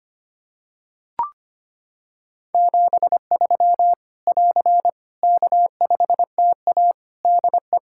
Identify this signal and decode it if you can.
Morse code, heard as a keyed tone, about 25 wpm, 700 Hz. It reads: 73 <AR> K5TA DE
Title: Practice QSO - Exchange 4 - Ending - 25wpm